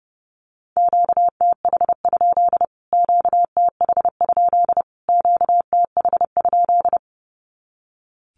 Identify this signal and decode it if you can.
Morse code, heard as a keyed tone, about 30 wpm, 700 Hz. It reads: QTH? QTH? QTH?
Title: Q-Codes Repeated 3x 30wpm